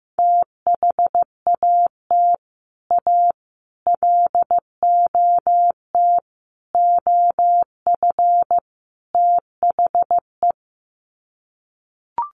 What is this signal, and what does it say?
Morse code, heard as a keyed tone, about 15 wpm, 700 Hz. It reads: THAT A LOT OF THE
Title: Sets of 5 Words Encore2 15wpm